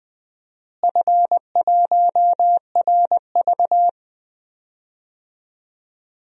Morse code, heard as a keyed tone, about 21 wpm, 700 Hz. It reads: F1RV
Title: DXCC-100 Call Signs - 4 Characters - Repeated 3x - 20wpm